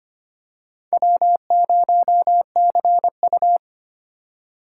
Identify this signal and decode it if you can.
Morse code, heard as a keyed tone, about 25 wpm, 700 Hz. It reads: W0CU
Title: USA Call Signs - 4 Characters - 3x Character Spacing - 25wpm